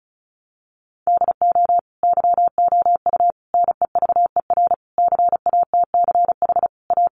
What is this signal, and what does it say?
Morse code, heard as a keyed tone, about 34 wpm, 700 Hz. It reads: DO YOU NEVER CATCH A
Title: Sentences from Top 1500 Words 35wpm